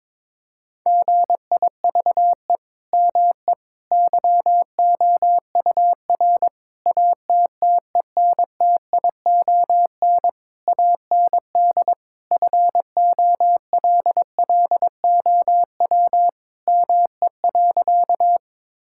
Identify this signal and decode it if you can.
Morse code, heard as a keyed tone, about 22 wpm, 700 Hz. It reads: GIVE ME YOUR ATTENTION AND FOLLOW ME.
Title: Sentences from Top 1000 Words - Review Part 6 - 22wpm